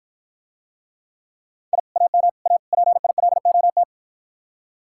Morse code, heard as a keyed tone, about 53 wpm, 700 Hz. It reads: I AM A PILOT